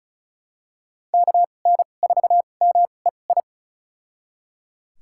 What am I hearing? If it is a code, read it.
Morse code, heard as a keyed tone, about 36 wpm, spaced as for 25 wpm, 700 Hz. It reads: KN4MEI
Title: USA Call Signs - 6 Characters - 2x Character Spacing - 35wpm